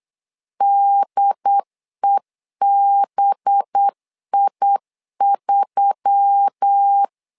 Morse code, heard as a keyed tone, about 8 wpm, 800 Hz. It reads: DEBI3
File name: LEZIONI / 5 LBVC / 5 LBVC 8wpm 2935